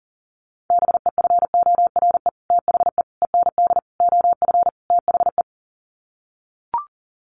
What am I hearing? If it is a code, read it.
Morse code, heard as a keyed tone, about 40 wpm, 700 Hz. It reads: BEFORE THE END OF THE
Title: Sets of 5 Words Encore 40wpm